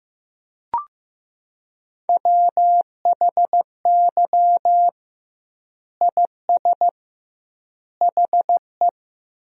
Morse code, heard as a keyed tone, about 16 wpm, 700 Hz. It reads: WHY IS HE
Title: 2x Word-Spacing - Sentences from Top 300 Words - 15wpm